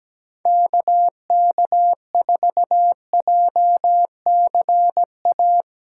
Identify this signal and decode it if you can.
Morse code, heard as a keyed tone, about 17 wpm, 700 Hz. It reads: KK4JCA
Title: USA Callsign DE Callsign - 6 Characters - 17wpm